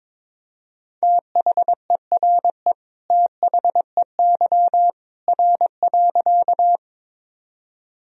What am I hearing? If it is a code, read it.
Morse code, heard as a keyed tone, about 22 wpm, 700 Hz. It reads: THERE THEY R.
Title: Sentences from Top 100 Words and Abbreviations 22wpm